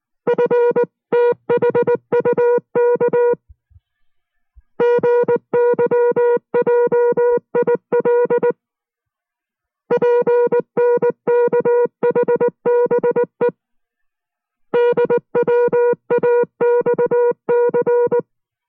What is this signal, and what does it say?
Morse code, heard as a keyed tone, about 19 wpm, 480 Hz. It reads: FTHUK GYJIL PNKHBE DWAXC